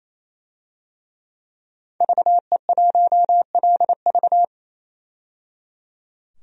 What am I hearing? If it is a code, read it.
Morse code, heard as a keyed tone, about 28 wpm, 700 Hz. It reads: VE1LV
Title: DXCC-100 Call Signs - 5 Characters - 28wpm